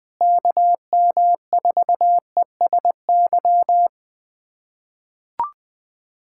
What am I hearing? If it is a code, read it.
Morse code, heard as a keyed tone, about 20 wpm, 700 Hz. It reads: KM4ESY
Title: USA Callsign DE Callsign 20wpm